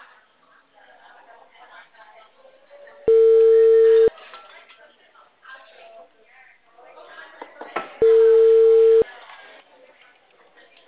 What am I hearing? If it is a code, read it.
Ringback tone (China, 450 Hz), with speech.
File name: out- -2015- - - .32898